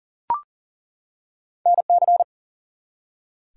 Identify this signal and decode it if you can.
Morse code, heard as a keyed tone, about 39 wpm, 700 Hz. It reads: NC
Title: US States and Canadian Provinces 40wpm